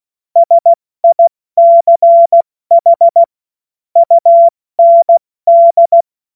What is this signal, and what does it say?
Morse code, heard as a keyed tone, about 16 wpm, spaced as for 14 wpm, 680 Hz. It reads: SICH UND
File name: Morsekurs / Witze / Witz 24. August 2021, 70 F80